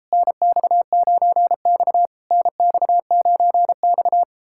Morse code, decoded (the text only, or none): NX9X NX9X